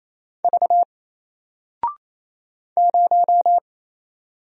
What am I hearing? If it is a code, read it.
Morse code, heard as a keyed tone, about 28 wpm, 700 Hz. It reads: V 0
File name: Single-Letter-Number-28wpm